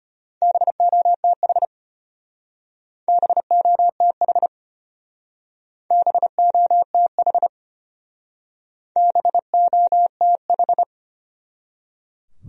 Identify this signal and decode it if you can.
Morse code, heard as a keyed tone, about 33 wpm, 700 Hz. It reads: BOTH BOTH BOTH BOTH